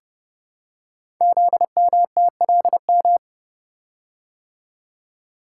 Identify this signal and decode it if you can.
Morse code, heard as a keyed tone, about 29 wpm, 700 Hz. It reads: ZMTLM
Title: Code Groups - 5 Letters - 30wpm